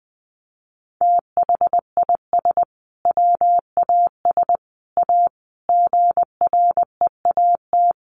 Morse code, heard as a keyed tone, about 20 wpm, 700 Hz. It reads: THIS WAS A GREAT
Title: Sentences from Top 1400 Words 20wpm